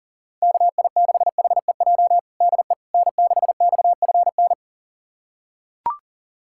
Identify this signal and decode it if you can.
Morse code, heard as a keyed tone, about 37 wpm, 700 Hz. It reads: KI6HEJ DE N6XFN
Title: USA Callsign DE Callsign 40wpm